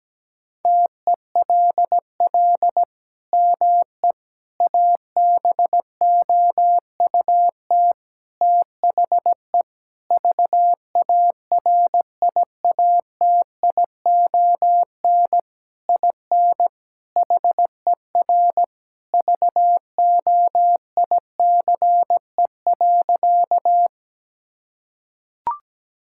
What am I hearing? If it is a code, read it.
Morse code, heard as a keyed tone, about 17 wpm, 700 Hz. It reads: TELL ME ABOUT THE VARIATION IN HER VOICE.
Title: Sentences from Top 1600 Words 17wpm